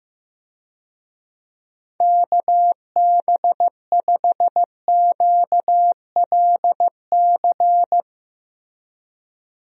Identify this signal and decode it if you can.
Morse code, heard as a keyed tone, about 15 wpm, 700 Hz. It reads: KB5QLC
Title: USA Call Signs - 6 Characters - Repeated 2x - 15wpm